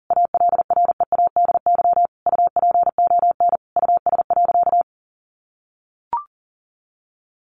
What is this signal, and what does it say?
Morse code, heard as a keyed tone, about 40 wpm, 700 Hz. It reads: ALREADY UPON US.
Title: Sentences from Top 1500 Words 40wpm